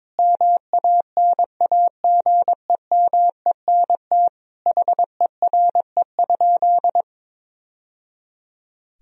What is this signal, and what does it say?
Morse code, heard as a keyed tone, about 22 wpm, 700 Hz. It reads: MANAGEMENT HERE?
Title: Sentences from Top 1000 Words - Review Part 4 - 22wpm